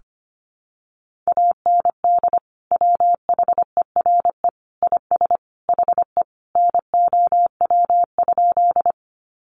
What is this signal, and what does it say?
Morse code, heard as a keyed tone, about 25 wpm, 700 Hz. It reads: AND WHERE IS HE NOW?
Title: Sentences from Top 200 Words 25wpm